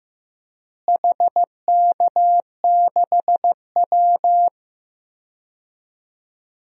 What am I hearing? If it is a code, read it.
Morse code, heard as a keyed tone, about 15 wpm, 700 Hz. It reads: HK6W
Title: DXCC-100 Call Signs - 4 Characters - Repeated 2x - 15wpm